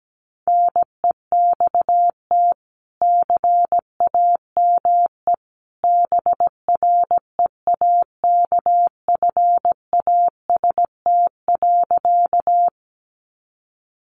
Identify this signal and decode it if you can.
Morse code, heard as a keyed tone, about 17 wpm, 700 Hz. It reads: NEXT CAME BREAKFAST.
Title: Sentences from Top 1500 Words 17wpm